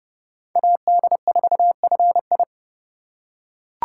Morse code, heard as a keyed tone, about 27 wpm, 700 Hz. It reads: AD4FI